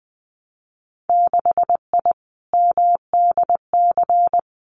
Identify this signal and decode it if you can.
Morse code, heard as a keyed tone, about 20 wpm, 700 Hz. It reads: 6I MDC
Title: Winter Field Day Exchanges Encore 20wpm